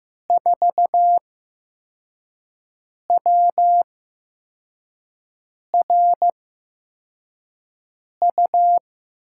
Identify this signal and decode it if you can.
Morse code, heard as a keyed tone, about 15 wpm, spaced as for 4 wpm, 700 Hz. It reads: 4WRU
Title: USA Call Signs - 6 Characters - 8x Character Spacing - 15wpm